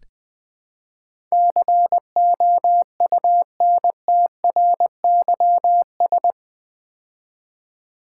Morse code, heard as a keyed tone, about 20 wpm, 700 Hz. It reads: COUNTRYS
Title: Build A Word - 4 Letters At A Time - 11-Letter Words - 20wpm